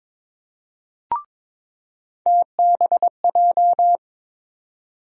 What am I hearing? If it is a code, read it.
Morse code, heard as a keyed tone, about 22 wpm, 700 Hz. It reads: TBJ
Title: USA Call Sign Postfixes - 3 Letters 22wpm